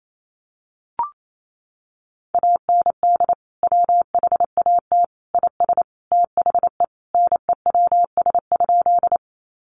Morse code, heard as a keyed tone, about 28 wpm, 700 Hz. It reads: AND WHAT IS THE NEWS?
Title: Sentences from Top 800 Words 28wpm